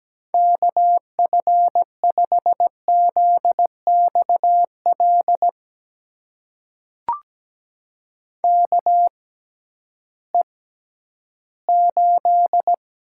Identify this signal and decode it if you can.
Morse code, heard as a keyed tone, about 17 wpm, spaced as for 11 wpm, 700 Hz. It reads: KF5ZXL KE8
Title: USA Call Signs - 6 Characters - 6x Character Spacing - 17wpm